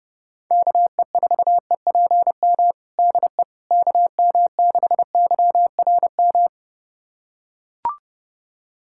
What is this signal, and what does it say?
Morse code, heard as a keyed tone, about 30 wpm, 700 Hz. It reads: KE4EPM DE KM6YRM
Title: USA Callsign DE Callsign - 6 Characters - 30wpm